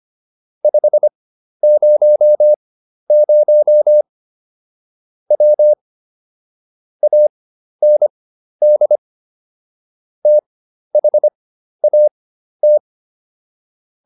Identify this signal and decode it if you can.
Morse code, heard as a keyed tone, about 25 wpm, spaced as for 12 wpm, 600 Hz. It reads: 500 W AND THAT